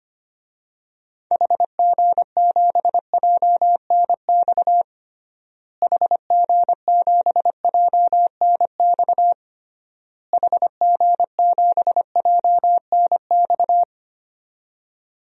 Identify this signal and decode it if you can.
Morse code, heard as a keyed tone, about 25 wpm, 700 Hz. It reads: HG7JNX HG7JNX HG7JNX
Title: DXCC-100 Call Signs - 6 Characters - Repeated 3x - 25wpm